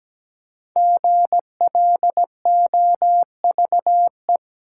Morse code, heard as a keyed tone, about 17 wpm, 700 Hz. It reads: GLOVE
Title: Five Letter Words - 17wpm